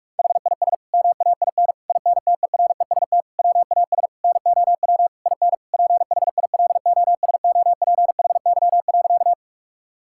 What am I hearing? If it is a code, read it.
Morse code, heard as a keyed tone, about 41 wpm, 700 Hz. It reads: HIS MAIN INTEREST WAS NOW IN PHILOSOPHY.
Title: Sentences from Top 1000 Words - Review Part 6 - 45wpm